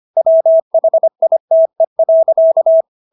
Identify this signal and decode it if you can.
Morse code, heard as a keyed tone, about 25 wpm, 650 Hz. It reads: WHITE.